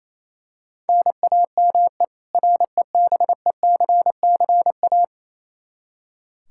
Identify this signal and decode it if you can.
Morse code, heard as a keyed tone, about 28 wpm, 700 Hz. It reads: NAME REBECCA
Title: K1USN SST QSO Practice 28wpm